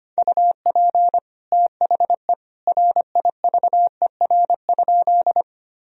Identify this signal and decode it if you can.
Morse code, heard as a keyed tone, about 25 wpm, 700 Hz. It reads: UP THE RIVER?
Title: Sentences from Top 1000 Words - Review Part 1 - 25wpm